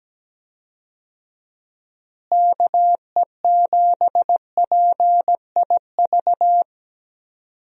Morse code, heard as a keyed tone, about 17 wpm, 700 Hz. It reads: KE7PIV
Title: USA Call Signs - 6 Characters - Repeated 3x - 17wpm